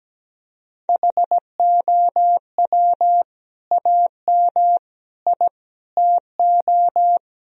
Morse code, heard as a keyed tone, about 17 wpm, 700 Hz. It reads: HOW AM I TO
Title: Sentences from Top 1000 Words - Review Part 4 - 17wpm